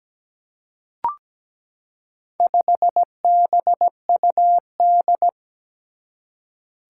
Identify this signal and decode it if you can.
Morse code, heard as a keyed tone, about 17 wpm, 700 Hz. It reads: 5BUD
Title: Four Challenging Letter-Number Combinations 17wpm